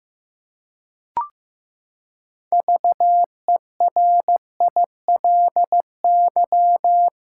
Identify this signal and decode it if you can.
Morse code, heard as a keyed tone, about 15 wpm, 700 Hz. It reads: VERILY
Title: Pangram Challenge v2 - 15wpm